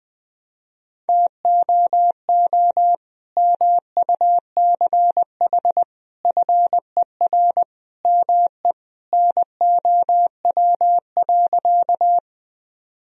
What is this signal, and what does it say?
Morse code, heard as a keyed tone, about 20 wpm, 700 Hz. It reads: TOO MUCH FER ME NOW.
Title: Sentences from Top 200 Words and Abbreviations 20wpm